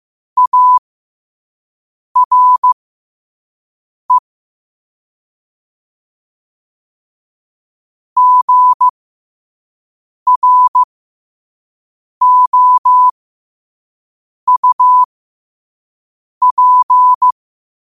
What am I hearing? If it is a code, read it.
Morse code, heard as a keyed tone, about 15 wpm, spaced as for 5 wpm, 1000 Hz. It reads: ARE GROUP
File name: qotdfiles / 5wpm